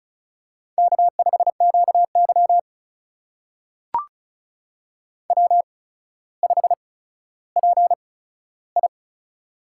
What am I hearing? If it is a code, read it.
Morse code, heard as a keyed tone, about 36 wpm, spaced as for 15 wpm, 700 Hz. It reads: K5QY W5PI
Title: USA Call Signs - 4 Characters - 8x Character Spacing - 35wpm